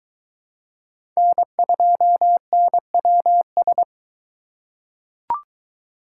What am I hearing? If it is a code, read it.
Morse code, heard as a keyed tone, about 22 wpm, 700 Hz. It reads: N2NWS